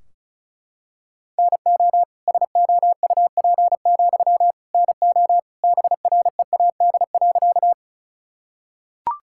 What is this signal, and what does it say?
Morse code, heard as a keyed tone, about 36 wpm, 700 Hz. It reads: NO SOUP, NO BREAD.
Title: Sentences from Top 1600 Words 35wpm